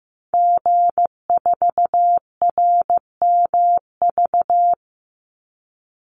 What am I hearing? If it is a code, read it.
Morse code, heard as a keyed tone, about 15 wpm, 700 Hz. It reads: G4RMV
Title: Call Signs Repeated 3x - 15wpm